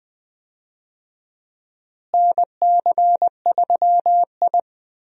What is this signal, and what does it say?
Morse code, heard as a keyed tone, about 21 wpm, 700 Hz. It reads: NC3I
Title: USA Call Signs - 4 Characters - Repeated 2x - 20wpm